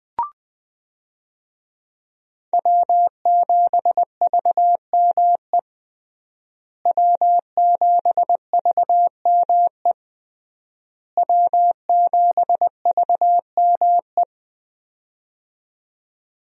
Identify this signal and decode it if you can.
Morse code, heard as a keyed tone, about 20 wpm, 700 Hz. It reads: W7VME W7VME W7VME
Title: USA Call Signs - 5 Characters - Repeated 3x - 20wpm